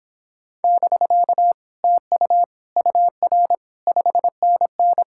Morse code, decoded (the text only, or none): <BK> TU UR 5NN